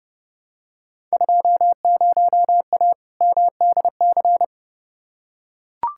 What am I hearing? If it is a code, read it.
Morse code, heard as a keyed tone, about 30 wpm, 700 Hz. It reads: 20A MDC
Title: Field Day Exchanges Encore - 30wpm